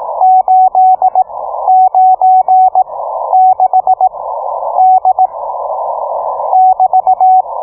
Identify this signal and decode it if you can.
Morse code, heard as a keyed tone, about 18 wpm, spaced as for 11 wpm, 750 Hz. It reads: 896D <BT>